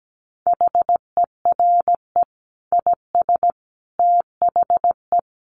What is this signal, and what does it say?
Morse code, heard as a keyed tone, about 17 wpm, 700 Hz. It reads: HERE IS THE
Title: Sentences from Top 300 Words 17wpm